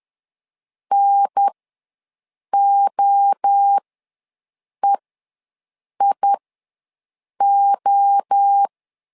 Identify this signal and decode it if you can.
Morse code, heard as a keyed tone, about 10 wpm, spaced as for 6 wpm, 800 Hz. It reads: NOEIO
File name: LEZIONI / 3 MWRO / 3 MWRO ws 10wpm 2310